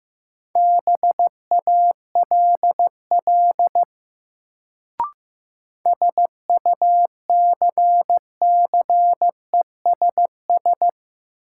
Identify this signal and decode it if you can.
Morse code, heard as a keyed tone, about 16 wpm, 700 Hz. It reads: BALL SUCCESS
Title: Sentences from Top 900 Words - New Word Drill - 15wpm